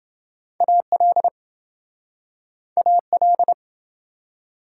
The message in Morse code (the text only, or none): AL AL